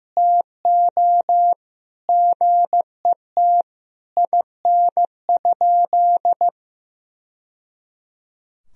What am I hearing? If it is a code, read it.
Morse code, heard as a keyed tone, about 16 wpm, 700 Hz. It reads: TO GET IN?
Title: Sentences from Top 1000 Words - Review Part 2 - 15wpm